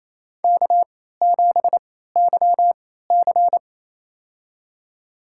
Morse code, heard as a keyed tone, about 28 wpm, spaced as for 16 wpm, 700 Hz. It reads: K7YC
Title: USA Call Signs - 3x Character Spacing - 28wpm